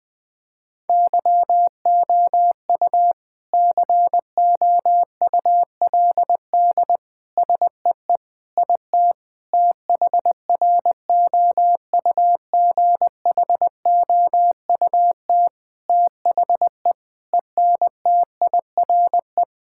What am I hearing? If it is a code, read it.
Morse code, heard as a keyed tone, about 20 wpm, 700 Hz. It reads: YOU COULD SEE IT THROUGHOUT THE ENTIRE
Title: Sentences from Top 1000 Words - Review Part 5 - 20wpm